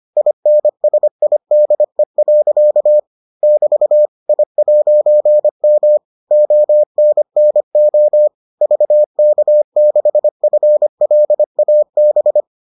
Morse code, heard as a keyed tone, about 25 wpm, 600 Hz. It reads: INSIDE. <BT> I'M ONNO VK6FLAB